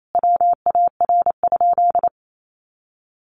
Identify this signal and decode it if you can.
Morse code, heard as a keyed tone, about 28 wpm, 700 Hz. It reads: WAR?
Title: Sentences from Top 1500 Words 28wpm